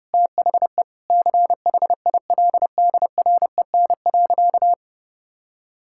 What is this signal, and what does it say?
Morse code, heard as a keyed tone, about 32 wpm, 700 Hz. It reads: THE CHILDREN.
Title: Sentences from Top 1000 Words - Review Part 6 - 30wpm